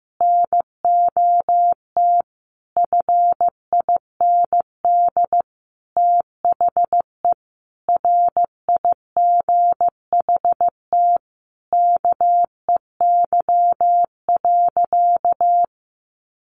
Morse code, heard as a keyed tone, about 14 wpm, 700 Hz. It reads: NOT FIND THE RIGHT KEY.
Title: Sentences from Top 600 Words 15wpm